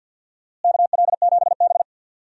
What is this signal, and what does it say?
Morse code, heard as a keyed tone, about 51 wpm, 700 Hz. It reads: KL7B
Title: DXCC-100 Call Signs - 4 Characters - Repeated 3x - 50wpm